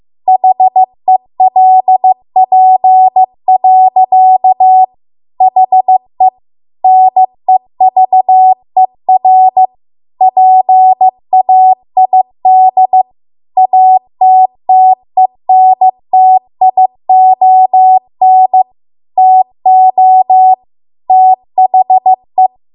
Morse code, heard as a keyed tone, about 15 wpm, 750 Hz. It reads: HELP. HE NEVER PAID ATTENTION TO THE